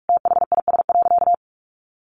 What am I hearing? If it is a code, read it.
Morse code, heard as a keyed tone, about 44 wpm, 700 Hz. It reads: THIS.
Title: Sentences from Top 1300 Words 45wpm